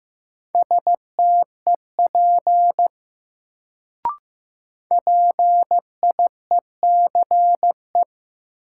Morse code, heard as a keyed tone, about 15 wpm, 700 Hz. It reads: STEP PIECE
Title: Sentences from Top 400 Words - New Word Drill - 15wpm